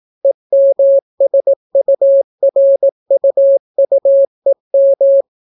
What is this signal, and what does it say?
Morse code, heard as a keyed tone, about 17 wpm, 550 Hz. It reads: EMSURUUEM